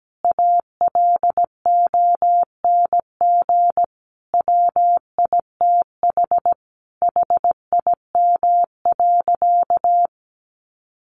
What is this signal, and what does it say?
Morse code, heard as a keyed tone, about 17 wpm, 700 Hz. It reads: ALONG WITH HIM.
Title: Sentences from Top 700 Words 17wpm